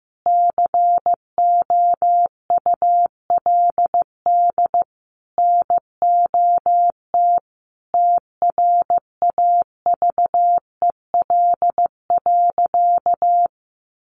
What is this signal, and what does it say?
Morse code, heard as a keyed tone, about 15 wpm, 700 Hz. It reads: COULD NOT TRAVEL.
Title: Sentences from Top 1300 Words 15wpm